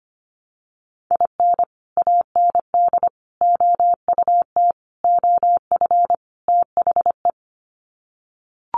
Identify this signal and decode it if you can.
Morse code, heard as a keyed tone, about 25 wpm, 700 Hz. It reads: IN AND OUT OF THE
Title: Sets of 5 Words 25wpm